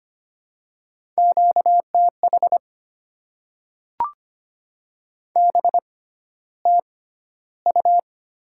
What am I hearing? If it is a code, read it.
Morse code, heard as a keyed tone, about 25 wpm, spaced as for 13 wpm, 690 Hz. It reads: QTH BTU
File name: Top 90 QSO Elements - 6x Character Spacing - 25wpm